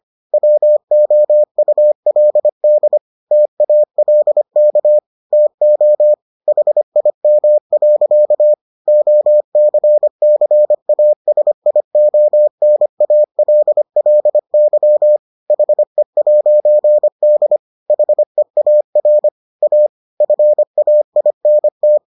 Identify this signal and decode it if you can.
Morse code, heard as a keyed tone, about 25 wpm, 600 Hz. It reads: WOULD TALK TO HIM. OCCASIONALLY HE'D HEAR A FAINT